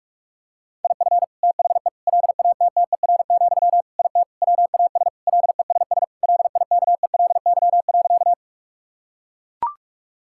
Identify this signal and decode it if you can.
Morse code, heard as a keyed tone, about 43 wpm, 700 Hz. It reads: IF THE LATTER, IT WAS LESS LIKELY.
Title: Sentences from Top 1600 Words 45wpm